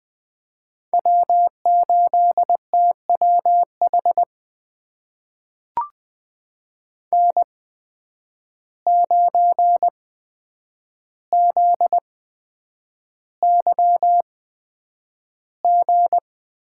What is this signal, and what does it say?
Morse code, heard as a keyed tone, about 20 wpm, spaced as for 9 wpm, 700 Hz. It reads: W8TWH N9ZYG
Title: USA Call Signs - 5 Characters - 8x Character Spacing - 20wpm